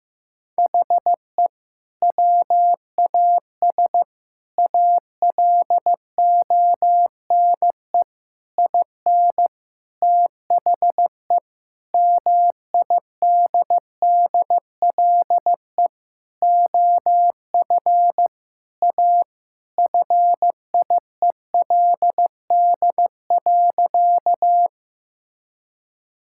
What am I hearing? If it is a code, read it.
Morse code, heard as a keyed tone, about 15 wpm, 700 Hz. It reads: HE WAS ALONE IN THE MIDDLE OF A FIELD.
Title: Sentences from Top 1000 Words - Review Part 6 - 15wpm